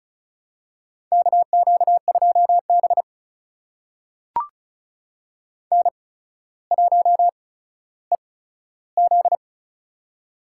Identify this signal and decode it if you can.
Morse code, heard as a keyed tone, about 36 wpm, spaced as for 15 wpm, 700 Hz. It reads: KQ2B N1EZ
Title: USA Call Signs - 4 Characters - 8x Character Spacing - 35wpm